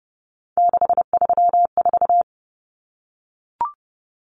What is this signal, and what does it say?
Morse code, heard as a keyed tone, about 30 wpm, 700 Hz. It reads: 634
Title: Common Cut Numbers - 3-Digit 30wpm